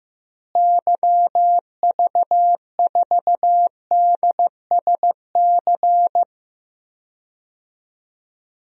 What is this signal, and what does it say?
Morse code, heard as a keyed tone, about 15 wpm, 700 Hz. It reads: YV4DSC
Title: DXCC-100 Call Signs - 6 Characters - Repeated 2x - 15wpm